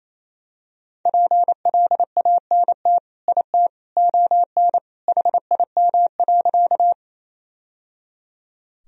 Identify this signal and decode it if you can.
Morse code, heard as a keyed tone, about 28 wpm, 700 Hz. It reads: PLANT IT ON HIM.
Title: Sentences from Top 1000 Words - Review Part 1 - 28wpm